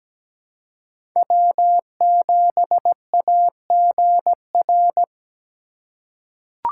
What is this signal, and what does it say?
Morse code, heard as a keyed tone, about 17 wpm, 700 Hz. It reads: W7AGR
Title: USA Call Signs - 5 Characters - 3x Character Spacing - 17wpm